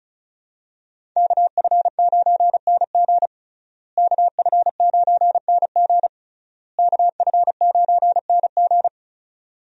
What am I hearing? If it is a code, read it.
Morse code, heard as a keyed tone, about 36 wpm, 700 Hz. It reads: KF9NG KF9NG KF9NG
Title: USA Call Signs - 5 Characters - Repeated 3x - 35wpm